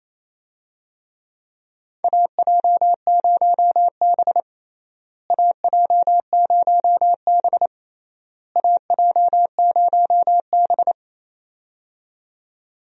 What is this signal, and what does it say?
Morse code, heard as a keyed tone, about 28 wpm, 700 Hz. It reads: AJ0B AJ0B AJ0B
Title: USA Call Signs - 4 Characters - Repeated 3x - 28wpm